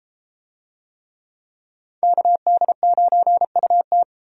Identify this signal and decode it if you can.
Morse code, heard as a keyed tone, about 33 wpm, 700 Hz. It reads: KD9UT